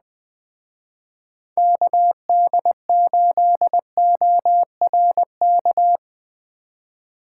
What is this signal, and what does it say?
Morse code, heard as a keyed tone, about 20 wpm, 700 Hz. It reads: KD8ORK